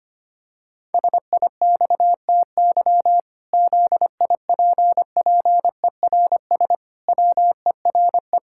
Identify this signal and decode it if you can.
Morse code, heard as a keyed tone, about 25 wpm, 700 Hz. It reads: SIXTY ZIPPERS WERE
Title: Pangram Challenge v2 - 25wpm